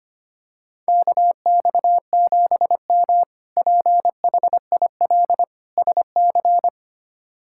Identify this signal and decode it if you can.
Morse code, heard as a keyed tone, about 25 wpm, 700 Hz. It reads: KX7M PHIL SC
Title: K1USN SST QSO Practice - 25wpm